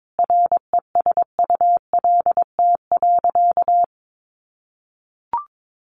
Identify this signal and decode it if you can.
Morse code, heard as a keyed tone, about 22 wpm, 700 Hz. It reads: RESULT.
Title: Sentences from Top 800 Words 22wpm